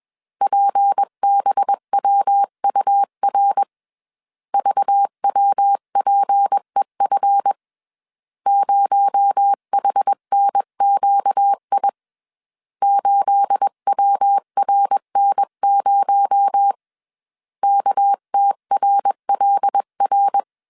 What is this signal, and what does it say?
Morse code, heard as a keyed tone, about 21 wpm, 800 Hz. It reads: PBWUR VWPEF 0HNQI 8WRN0 KTRLR
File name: CW / DET / 1 9 1500C S B 20wpm 2313